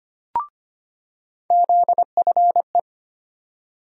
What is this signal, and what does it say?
Morse code, heard as a keyed tone, about 25 wpm, 700 Hz. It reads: ZFE
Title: USA Call Sign Postfixes - 3 Letters 25wpm